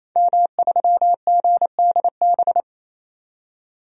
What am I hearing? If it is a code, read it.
Morse code, heard as a keyed tone, about 28 wpm, 700 Hz. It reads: M3GDB
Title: DXCC-100 Call Signs - 5 Characters - 28wpm